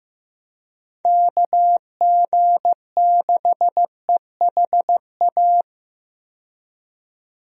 Morse code, heard as a keyed tone, about 15 wpm, 700 Hz. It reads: KG6EHA
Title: USA Call Signs - 6 Characters - 15wpm